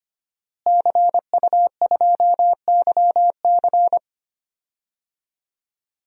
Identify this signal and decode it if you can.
Morse code, heard as a keyed tone, about 25 wpm, 700 Hz. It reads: CU2YC
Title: DXCC-100 Call Signs - 5 Characters - Repeated 2x - 25wpm